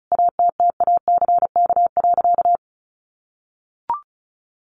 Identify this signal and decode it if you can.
Morse code, heard as a keyed tone, about 34 wpm, 700 Hz. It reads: ATTACK.
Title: Sentences from Top 1500 Words 35wpm